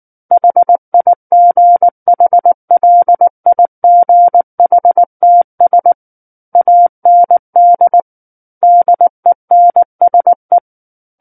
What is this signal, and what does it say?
Morse code, heard as a keyed tone, about 19 wpm, 700 Hz. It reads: HIGHLIGHTS AND DENSE
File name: Book One / 10-Chapter 10 - 19 WPM